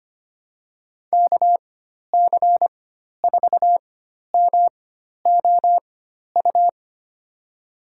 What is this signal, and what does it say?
Morse code, heard as a keyed tone, about 25 wpm, spaced as for 12 wpm, 700 Hz. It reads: KC4MOU
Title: USA Call Signs - 6 Characters - 4x Character Spacing - 25wpm